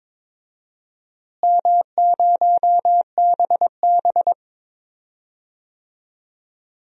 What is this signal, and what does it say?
Morse code, heard as a keyed tone, about 22 wpm, 700 Hz. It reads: M0BB
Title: DXCC-100 Call Signs - 4 Characters - Repeated 3x - 22wpm